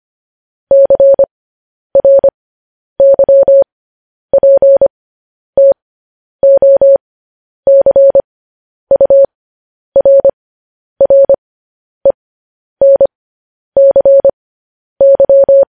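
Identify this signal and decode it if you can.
Morse code, heard as a keyed tone, about 25 wpm, spaced as for 10 wpm, 550 Hz. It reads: CRYPTOCURRENCY
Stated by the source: Morse code news headlines practice at 10 wpm.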